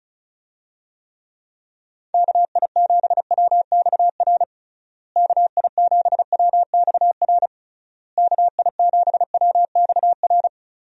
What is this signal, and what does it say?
Morse code, heard as a keyed tone, about 36 wpm, 700 Hz. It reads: KI7WXR KI7WXR KI7WXR
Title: USA Call Signs - 6 Characters - Repeated 3x - 35wpm